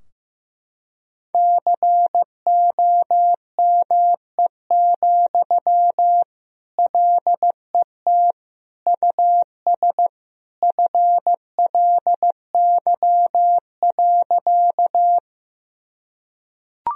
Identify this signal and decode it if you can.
Morse code, heard as a keyed tone, about 15 wpm, 700 Hz. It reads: COME, LET US FLY.
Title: Sentences from Top 1000 Words - Review Part 2 - 15wpm